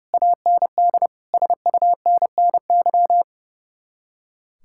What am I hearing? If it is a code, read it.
Morse code, heard as a keyed tone, about 40 wpm, 700 Hz. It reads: AND SUNNY